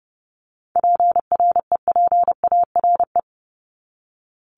Morse code, heard as a keyed tone, about 30 wpm, 700 Hz. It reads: PREPARE
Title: ED Suffix - Top 100 Words with Base Words 30wpm